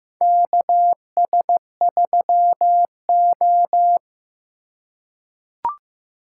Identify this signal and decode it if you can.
Morse code, heard as a keyed tone, about 15 wpm, 700 Hz. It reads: KS3O
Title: USA Call Signs - 4 Characters - 2x Character Spacing - 15wpm